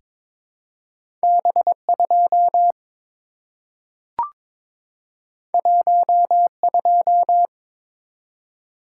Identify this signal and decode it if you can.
Morse code, heard as a keyed tone, about 23 wpm, 700 Hz. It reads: B2 12
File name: practice / 22wpm / Two Letter-Number Combinations - 22wpm